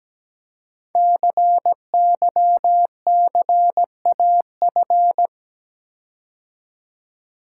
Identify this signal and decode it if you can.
Morse code, heard as a keyed tone, about 17 wpm, 700 Hz. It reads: CYCAF
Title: Code Groups - 5 Letters - 17wpm